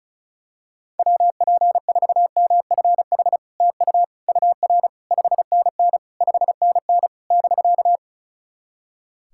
Morse code, heard as a keyed tone, about 36 wpm, 700 Hz. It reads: WP4MFH TU UR 5NN 5NN <BK>
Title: SOTA QSO Practice 35wpm